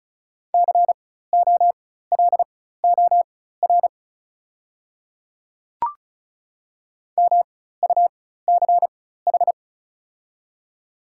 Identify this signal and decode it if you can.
Morse code, heard as a keyed tone, about 34 wpm, spaced as for 16 wpm, 700 Hz. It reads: COLOR MUCH
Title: 4x Character Spacing - Top 300 Words - 35wpm